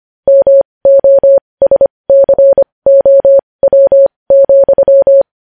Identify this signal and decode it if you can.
Morse code, heard as a keyed tone, about 25 wpm, spaced as for 20 wpm, 550 Hz. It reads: MOSCOW,
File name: podcast / News headlines-1x-25x20